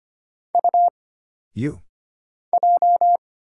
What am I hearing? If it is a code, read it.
Morse code, heard as a keyed tone, about 25 wpm, 700 Hz. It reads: U J
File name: Single Character (Letter:Number:Punctuation:Prosign) - Mind-Melt - 25wpm